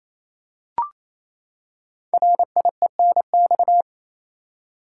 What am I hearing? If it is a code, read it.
Morse code, heard as a keyed tone, about 27 wpm, 700 Hz. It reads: RIENX
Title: Code Groups - 5 Letters - 28wpm